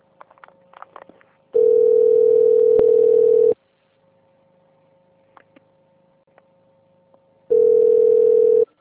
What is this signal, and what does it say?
Ringback tone (North America), 440 + 480 Hz.
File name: out- - - - - .75139